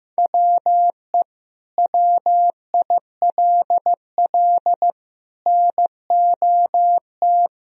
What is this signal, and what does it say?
Morse code, heard as a keyed tone, about 16 wpm, 700 Hz. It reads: WE WILL NOT
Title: Sentences from Top 1000 Words - Review Part 1 - 15wpm